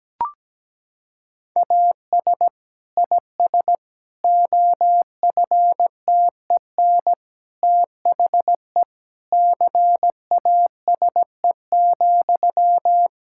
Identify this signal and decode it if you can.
Morse code, heard as a keyed tone, about 17 wpm, 700 Hz. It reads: AS IS OFTEN THE CASE,
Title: Sentences from Top 1000 Words - Review Part 5 - 17wpm